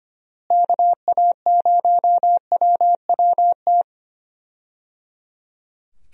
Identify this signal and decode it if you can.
Morse code, heard as a keyed tone, about 25 wpm, 700 Hz. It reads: KA0WWT
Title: USA Call Signs - 6 Characters - Repeated 3x - 25wpm